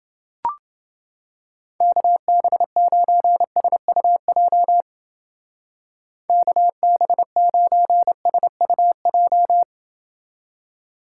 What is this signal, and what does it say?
Morse code, heard as a keyed tone, about 29 wpm, 700 Hz. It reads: KB9SUJ KB9SUJ